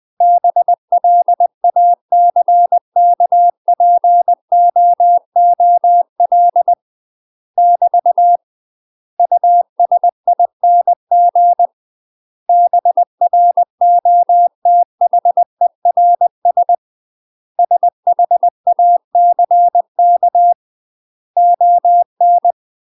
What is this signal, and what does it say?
Morse code, heard as a keyed tone, about 20 wpm, 700 Hz. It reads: BLACKPOOL <BT> USING BROTHERS SHACK ON
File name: UK MASTER QSOs TRAINING / 20WPM / UK MASTER QSOs 0015